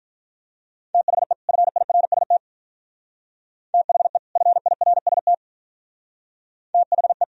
Morse code, heard as a keyed tone, about 42 wpm, 700 Hz. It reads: THE FIRST THE FIRST THE
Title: Speed-Racing - Sets of 2 Words 35wpm